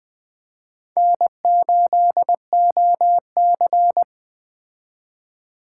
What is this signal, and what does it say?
Morse code, heard as a keyed tone, about 20 wpm, 700 Hz. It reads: N8OC
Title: Speed Racing - USA 1x2 Call Signs 17wpm